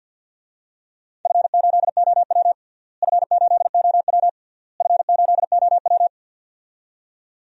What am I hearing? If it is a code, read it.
Morse code, heard as a keyed tone, about 47 wpm, 700 Hz. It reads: F8OW F8OW F8OW
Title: DXCC-100 Call Signs - 4 Characters - Repeated 3x - 50wpm